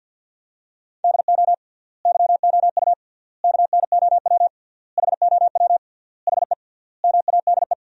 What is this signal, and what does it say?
Morse code, heard as a keyed tone, about 53 wpm, 700 Hz. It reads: DO YOU KNOW HOW HE MADE
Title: 3x Word-Spacing - Sentences from Top 300 Words - 50wpm